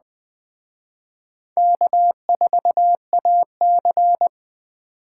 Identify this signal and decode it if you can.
Morse code, heard as a keyed tone, about 20 wpm, 700 Hz. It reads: K4AC
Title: Speed Racing - USA 1x2 Call Signs 17wpm